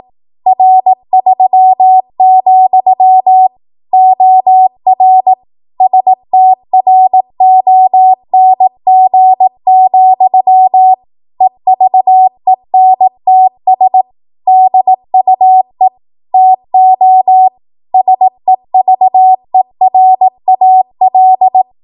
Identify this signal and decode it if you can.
Morse code, heard as a keyed tone, about 18 wpm, 750 Hz. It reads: R3, OR STRONG, EVENTS DUE TO SEVERAL
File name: Morse / 18WPM